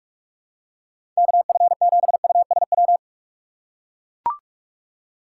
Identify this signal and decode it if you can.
Morse code, heard as a keyed tone, about 44 wpm, 700 Hz. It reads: KF7USW
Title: USA Call Signs - 6 Characters - 8x Character Spacing - 45wpm